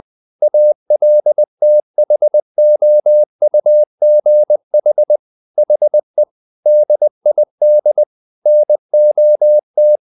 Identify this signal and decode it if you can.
Morse code, heard as a keyed tone, about 20 wpm, 600 Hz. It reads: ALTHOUGH HE DID NOT
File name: the old man and the sea / the old man and the sea part 1 20wpm